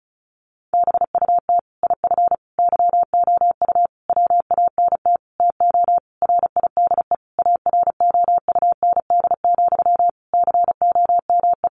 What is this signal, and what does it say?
Morse code, heard as a keyed tone, about 34 wpm, 700 Hz. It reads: BUT IF YOU WANT TO RIDE AROUND, COME
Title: Sentences from Top 1200 Words 35wpm